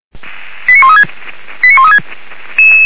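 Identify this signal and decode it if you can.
Five-tone selective call (ZVEI), heard as quick runs of five tones, tones of about 70 ms. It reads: 89136 89136
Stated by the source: Five-tone ZVEI selective call received on an SDR.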